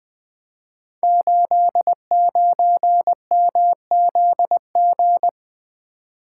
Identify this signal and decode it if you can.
Morse code, heard as a keyed tone, about 20 wpm, 700 Hz. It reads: 89MZG
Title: Code Groups - 5 Letter:Numbers - 20wpm